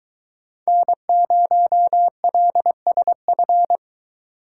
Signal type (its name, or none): Morse code